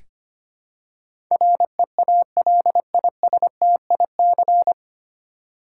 Morse code, heard as a keyed tone, about 25 wpm, 700 Hz. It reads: REALISTIC
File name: Nine Letter Words - 25wpm